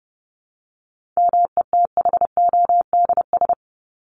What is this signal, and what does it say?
Morse code, heard as a keyed tone, about 30 wpm, 700 Hz. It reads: METHODS
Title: ES:S Suffix - Top 100 Words with Base Words 30wpm